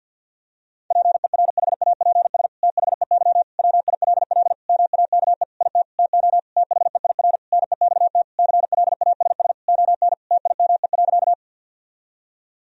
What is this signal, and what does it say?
Morse code, heard as a keyed tone, about 51 wpm, 700 Hz. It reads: PERHAPS THEY WILL MAKE IT TO THEIR NEXT CLASS ON TIME.
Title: Sentences from Top 1000 Words - Review Part 2 - 50wpm